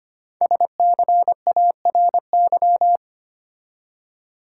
Morse code, heard as a keyed tone, about 25 wpm, 700 Hz. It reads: SCARY
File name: 5 Letter Words - First 3 Letters Given - 25wpm